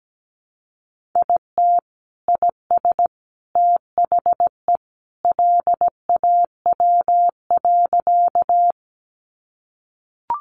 Morse code, heard as a keyed tone, about 17 wpm, 700 Hz. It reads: IT IS THE LAW.
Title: Sentences from Top 600 Words 17wpm